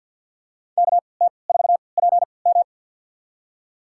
Morse code, heard as a keyed tone, about 52 wpm, spaced as for 29 wpm, 700 Hz. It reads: KT4PM
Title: USA Call Signs - 5 Characters - 3x Character Spacing - 50wpm